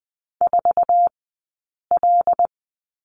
Morse code, heard as a keyed tone, about 20 wpm, 700 Hz. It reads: 4 L
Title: 106 Lesson 18d - Letter m - Two Characters 20wpm